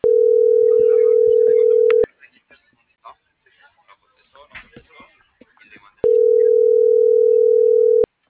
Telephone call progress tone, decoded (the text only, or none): ringback tone